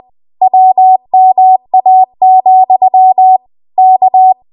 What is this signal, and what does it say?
Morse code, heard as a keyed tone, about 20 wpm, 750 Hz. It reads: WMA, K
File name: Archive / 20 WPM / 20WPM